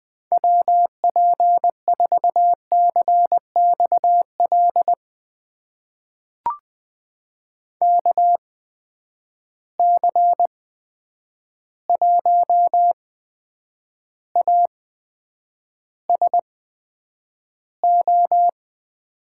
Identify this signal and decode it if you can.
Morse code, heard as a keyed tone, about 20 wpm, spaced as for 9 wpm, 700 Hz. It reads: WP4CXL KC1ASO
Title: USA Call Signs - 6 Characters - 8x Character Spacing - 20wpm